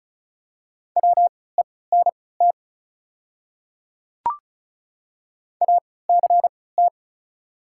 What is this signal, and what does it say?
Morse code, heard as a keyed tone, about 36 wpm, spaced as for 20 wpm, 700 Hz. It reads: WENT ACT
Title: 3x Character Spacing - Top 200 Words - 35wpm